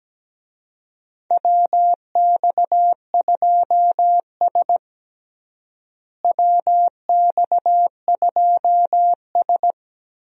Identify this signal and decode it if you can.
Morse code, heard as a keyed tone, about 17 wpm, 700 Hz. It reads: WX2S WX2S
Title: USA Call Signs - 4 Characters - Repeated 3x - 17wpm